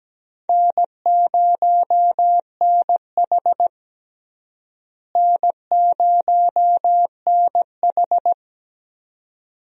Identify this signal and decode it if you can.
Morse code, heard as a keyed tone, about 17 wpm, 700 Hz. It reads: N0NH N0NH
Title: USA Call Signs - 4 Characters - Repeated 2x - 17wpm